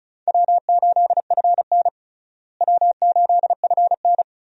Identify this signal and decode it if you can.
Morse code, heard as a keyed tone, about 36 wpm, 700 Hz. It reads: W8FN W8FN
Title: USA Call Signs - 4 Characters - Repeated 3x - 35wpm